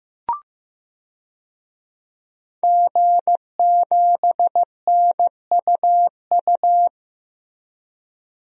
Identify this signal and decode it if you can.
Morse code, heard as a keyed tone, about 15 wpm, 700 Hz. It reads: G7NUU
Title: DXCC-100 Call Signs - 5 Characters - Repeated 2x - 15wpm